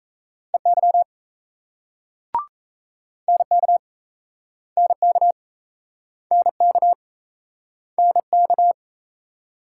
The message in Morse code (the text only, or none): EY NK NK NK NK